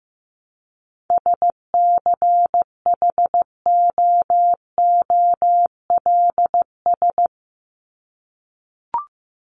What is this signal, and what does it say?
Morse code, heard as a keyed tone, about 15 wpm, 700 Hz. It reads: SCHOOLS
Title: ES:S Suffix - Top 100 Words with Base Words 15wpm